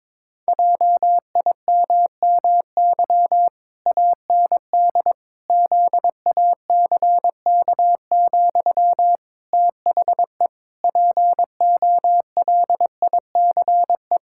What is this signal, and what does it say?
Morse code, heard as a keyed tone, about 22 wpm, 700 Hz. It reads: JIMMY AND ZACK, THE POLICE
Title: Pangram Challenge v2 - 22wpm